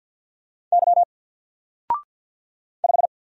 Morse code, heard as a keyed tone, about 48 wpm, 700 Hz. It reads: Y 5
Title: Single Letter-Number Alt 50wpm